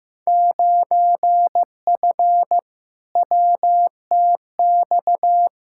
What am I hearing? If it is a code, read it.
Morse code, heard as a keyed tone, about 15 wpm, 700 Hz. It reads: 9F WTX
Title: Field Day Exchanges Encore - 15wpm